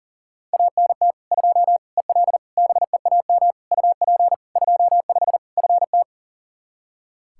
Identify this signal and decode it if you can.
Morse code, heard as a keyed tone, about 37 wpm, 700 Hz. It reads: ANT 2 EL BEAM UP 25 FT
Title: K1USN SST QSO Practice 40wpm